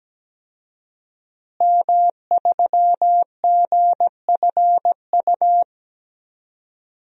Morse code, heard as a keyed tone, about 17 wpm, 700 Hz. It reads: M3GFU
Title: DXCC-100 Call Signs - 5 Characters - Repeated 2x - 17wpm